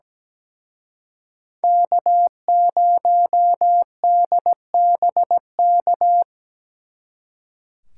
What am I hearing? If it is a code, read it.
Morse code, heard as a keyed tone, about 20 wpm, 700 Hz. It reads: K0DBK
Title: Speed Racing - USA 1x3 Call Signs 17wpm